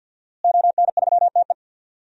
Morse code, heard as a keyed tone, about 48 wpm, 700 Hz. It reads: ON3TE
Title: DXCC-100 Call Signs - 5 Characters - Repeated 2x - 50wpm